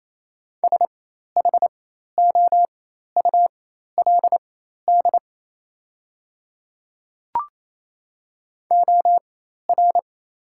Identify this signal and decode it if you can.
Morse code, heard as a keyed tone, about 28 wpm, spaced as for 13 wpm, 700 Hz. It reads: SHOULD OR
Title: 4x Character Spacing - Top 300 Words - 28wpm